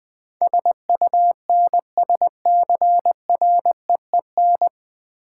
Morse code, heard as a keyed tone, about 20 wpm, 700 Hz. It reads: SUNSCREEN
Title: Easy Words with Hints - 20wpm